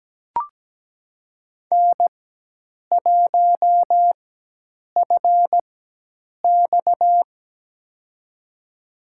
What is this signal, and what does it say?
Morse code, heard as a keyed tone, about 17 wpm, spaced as for 8 wpm, 700 Hz. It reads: N1FX
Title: USA Call Signs - 4 Characters - 4x Character Spacing - 17wpm